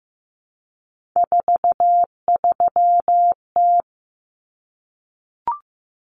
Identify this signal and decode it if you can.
Morse code, heard as a keyed tone, about 15 wpm, 700 Hz. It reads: 43T
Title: Common Cut Numbers - 3-Digit 15wpm